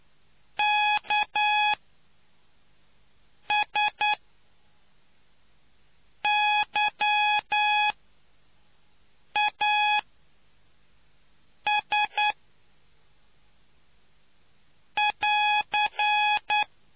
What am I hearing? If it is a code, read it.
Morse code, heard as a keyed tone, about 9 wpm, spaced as for 4 wpm, 840 Hz. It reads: KSYAS<AR>